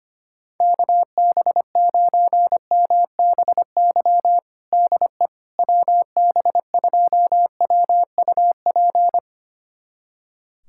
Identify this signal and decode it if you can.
Morse code, heard as a keyed tone, about 25 wpm, 700 Hz. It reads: KB9MBY DE WB2WUP
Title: USA Callsign DE Callsign - 6 Characters - 25wpm